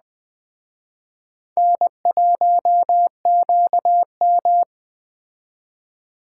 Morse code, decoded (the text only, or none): N1QM